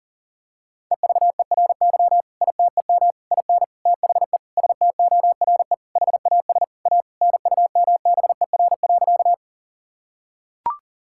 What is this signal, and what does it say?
Morse code, heard as a keyed tone, about 40 wpm, 700 Hz. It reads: EVERY ITEM IN THE STORE HAS A NUMBER.
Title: Sentences from Top 1000 Words - Review Part 6 - 40wpm